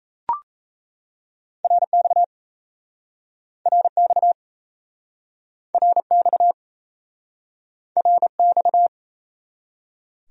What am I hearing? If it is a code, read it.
Morse code, heard as a keyed tone, about 38 wpm, 700 Hz. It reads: RX RX RX RX